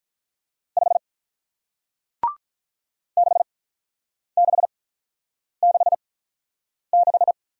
Morse code, heard as a keyed tone, about 48 wpm, spaced as for 9 wpm, 700 Hz. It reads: 5 6666